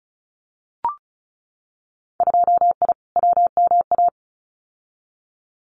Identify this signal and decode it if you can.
Morse code, heard as a keyed tone, about 34 wpm, 700 Hz. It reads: 2I WMA
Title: Winter Field Day Exchanges Encore 35wpm